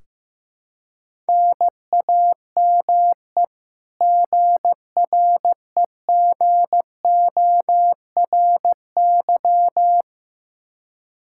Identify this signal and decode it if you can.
Morse code, heard as a keyed tone, about 15 wpm, 700 Hz. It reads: NAME GREGORY
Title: Practice QSO - Exchange 1 - Beginning - 15wpm